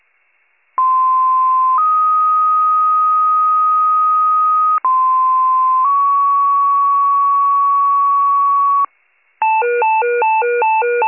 Two-tone page, heard as a one-second tone, then a three-second tone, about 1040 then 1250 Hz, about 1040 then 1120 Hz. It is heard from 0.8 to 8.9 s.